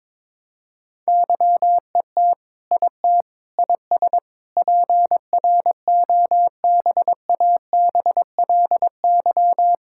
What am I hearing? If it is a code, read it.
Morse code, heard as a keyed tone, about 22 wpm, 700 Hz. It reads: YET IT IS PROBABLY
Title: Sentences from Top 1600 Words 22wpm